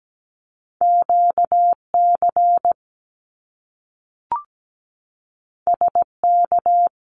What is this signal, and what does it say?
Morse code, heard as a keyed tone, about 17 wpm, 700 Hz. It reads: QC SK
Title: Canadian Provinces and Territories 17wpm